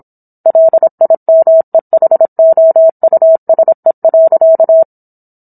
Morse code, heard as a keyed tone, about 27 wpm, 650 Hz. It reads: LIMEHOUSE.